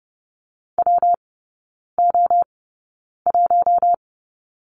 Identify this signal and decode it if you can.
Morse code, heard as a keyed tone, about 30 wpm, spaced as for 9 wpm, 700 Hz. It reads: WO1
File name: Lesson 19e - Letter w - Three Characters 30wpm